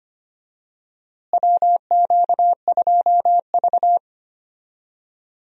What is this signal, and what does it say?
Morse code, heard as a keyed tone, about 25 wpm, 700 Hz. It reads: WQ2V
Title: USA Call Signs - 4 Characters - Repeated 2x - 25wpm